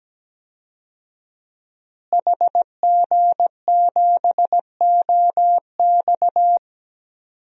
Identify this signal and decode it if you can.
Morse code, heard as a keyed tone, about 17 wpm, 700 Hz. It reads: HG7OX
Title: DXCC-100 Call Signs - 5 Characters - Repeated 2x - 17wpm